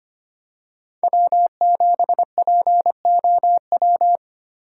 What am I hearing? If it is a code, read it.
Morse code, heard as a keyed tone, about 25 wpm, 700 Hz. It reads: W7POW
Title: USA Call Signs - 4x Character Spacing - 25wpm